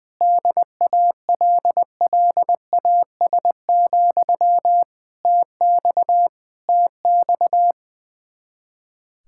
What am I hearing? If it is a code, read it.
Morse code, heard as a keyed tone, about 20 wpm, 700 Hz. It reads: DALLAS, TX TX
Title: Practice QSO - Exchange 1 - Beginning - 20wpm